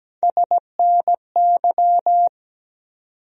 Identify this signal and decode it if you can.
Morse code, heard as a keyed tone, about 17 wpm, 700 Hz. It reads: SNY
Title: USA Call Sign Postfixes - 3 Letters 17wpm